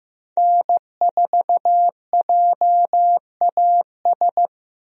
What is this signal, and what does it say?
Morse code, heard as a keyed tone, about 15 wpm, 700 Hz. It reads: N4JAS
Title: USA Call Signs - 5 Characters - Repeated 3x - 15wpm